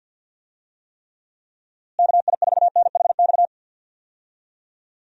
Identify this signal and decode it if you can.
Morse code, heard as a keyed tone, about 50 wpm, 700 Hz. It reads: KI4NHX
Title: USA Call Signs - 6 Characters - Repeated 2x - 50wpm